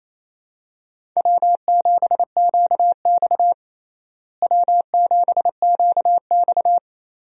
Morse code, heard as a keyed tone, about 28 wpm, 700 Hz. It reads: W7QX W7QX
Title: USA Call Signs - 4 Characters - Repeated 2x - 28wpm